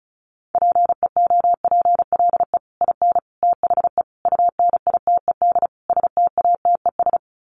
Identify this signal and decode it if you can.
Morse code, heard as a keyed tone, about 34 wpm, 700 Hz. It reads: PEOPLE IN THE UNITED STATES
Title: Sets of 5 Words Encore 35wpm